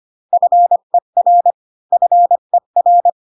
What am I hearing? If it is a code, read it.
Morse code, heard as a keyed tone, about 25 wpm, 700 Hz. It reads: FER FER